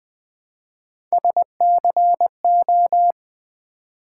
Morse code, heard as a keyed tone, about 20 wpm, 700 Hz. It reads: SCO
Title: Build A Word - 1 Letter At A Time - 3 to 5-Letter Words - 20wpm